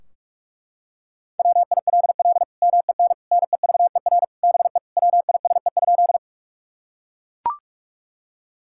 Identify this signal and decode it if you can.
Morse code, heard as a keyed tone, about 42 wpm, 700 Hz. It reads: WILL MEN NEVER BE WISE?
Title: Sentences from Top 1600 Words 45wpm